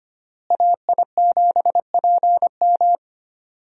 Morse code, heard as a keyed tone, about 25 wpm, 700 Hz. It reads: AI7PM
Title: USA Call Signs - 5 Characters - Repeated 3x - 25wpm